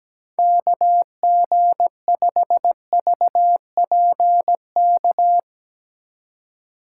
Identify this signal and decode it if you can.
Morse code, heard as a keyed tone, about 17 wpm, 700 Hz. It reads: KG5VPK
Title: USA Call Signs - 2x Character Spacing - 17wpm